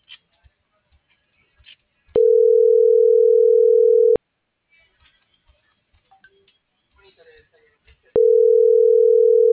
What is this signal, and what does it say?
Ringback tone (North America), 440 + 480 Hz.